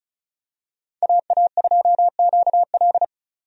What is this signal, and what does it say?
Morse code, heard as a keyed tone, about 36 wpm, 700 Hz. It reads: AA2QL
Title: USA Call Signs - 5 Characters - 35wpm